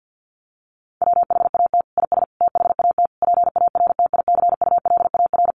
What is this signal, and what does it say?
Morse code, heard as a keyed tone, about 50 wpm, 700 Hz. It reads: WHAT IS THAT PARTICULAR
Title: Sentences from Top 1500 Words 50wpm